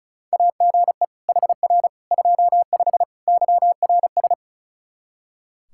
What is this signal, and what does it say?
Morse code, heard as a keyed tone, about 34 wpm, 700 Hz. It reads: AGE HR 25 YRS
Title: K1USN SST QSO Practice 35wpm